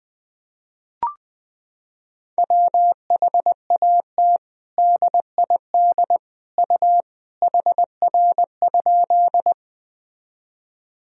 Morse code, heard as a keyed tone, about 20 wpm, 700 Hz. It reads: WHAT DID U HR?
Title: Sentences from Top 300 Words and Abbreviations 20wpm